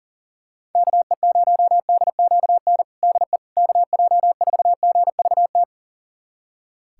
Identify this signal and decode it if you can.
Morse code, heard as a keyed tone, about 40 wpm, 700 Hz. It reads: KE0DQN DE KJ4GVT
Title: USA Callsign DE Callsign - 6 Characters - 40wpm